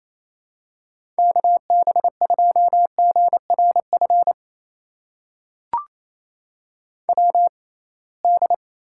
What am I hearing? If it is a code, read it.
Morse code, heard as a keyed tone, about 28 wpm, spaced as for 21 wpm, 700 Hz. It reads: KB2GRF WD
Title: USA Call Signs - 6 Characters - 6x Character Spacing - 28wpm